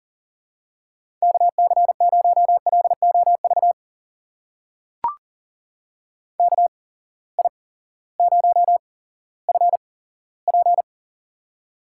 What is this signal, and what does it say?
Morse code, heard as a keyed tone, about 39 wpm, spaced as for 18 wpm, 700 Hz. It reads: KC0LOV KI0FP
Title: USA Call Signs - 8x Character Spacing - 40wpm